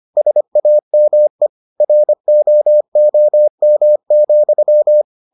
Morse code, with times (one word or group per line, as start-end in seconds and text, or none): 0.2-1.5 SAME
1.8-5.0 ROOM,